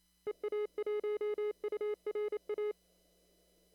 Morse code, heard as a keyed tone, about 29 wpm, 410 Hz. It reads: EA1URA
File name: CW / kt5j wpx cw-op-k5pi